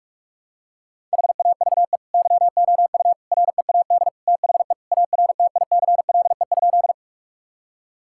Morse code, heard as a keyed tone, about 41 wpm, 700 Hz. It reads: HAVE YOU READ THE ARTICLE?
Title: Sentences from Top 1000 Words - Review Part 5 - 45wpm